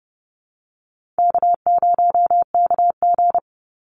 Morse code, heard as a keyed tone, about 30 wpm, 700 Hz. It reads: K0KG
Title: Easy Call Signs Repeated 2x - 30wpm